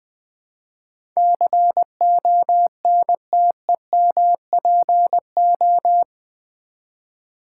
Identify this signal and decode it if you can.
Morse code, heard as a keyed tone, about 20 wpm, 700 Hz. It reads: CONTEMPO
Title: Build A Word - 4 Letters At A Time - 12-Letter Words - 20wpm